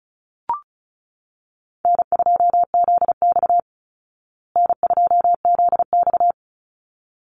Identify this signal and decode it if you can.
Morse code, heard as a keyed tone, about 34 wpm, 700 Hz. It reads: N2ZX N2ZX
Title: Easy Call Signs Repeated 2x - 35wpm